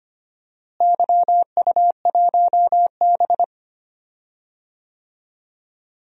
Morse code, heard as a keyed tone, about 25 wpm, 700 Hz. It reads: YU1B
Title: DXCC-100 Call Signs - 4 Characters - Repeated 3x - 25wpm